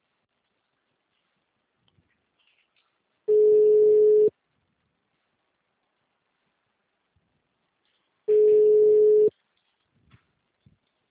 Ringback tone (Europe, 425 Hz).
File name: out- -2001- - - .4